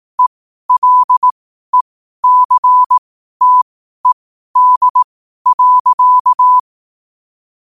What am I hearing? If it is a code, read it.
Morse code, heard as a keyed tone, about 18 wpm, spaced as for 12 wpm, 1000 Hz. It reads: ELECTED.
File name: qotdfiles / 12wpm